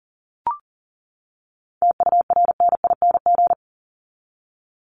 Morse code, heard as a keyed tone, about 40 wpm, 700 Hz. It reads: TURNING
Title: ING Suffix - Top 100 Words with Base Words 40wpm